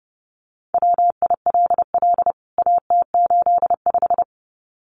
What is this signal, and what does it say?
Morse code, heard as a keyed tone, about 30 wpm, 700 Hz. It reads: WILL AT85
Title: CWT Exchange 30wpm